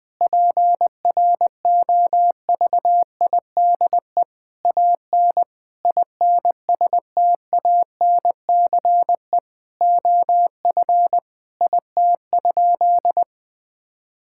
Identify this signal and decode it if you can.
Morse code, heard as a keyed tone, about 20 wpm, 700 Hz. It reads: PROVIDE AN INSTANCE OF IT?
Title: Sentences from Top 1000 Words - Review Part 5 - 20wpm